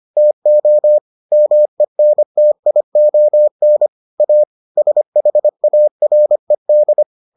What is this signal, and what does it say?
Morse code, heard as a keyed tone, about 25 wpm, 600 Hz. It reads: TO MENTION A SHARED